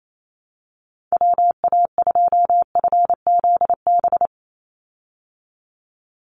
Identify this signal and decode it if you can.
Morse code, heard as a keyed tone, about 28 wpm, 700 Hz. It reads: WA2FZB
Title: Call Signs Repeated 2x - 28wpm